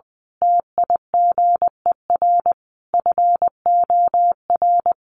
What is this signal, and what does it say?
Morse code, heard as a keyed tone, about 20 wpm, 700 Hz. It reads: TIGER FOR
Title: A Sound Of Thunder 20wpm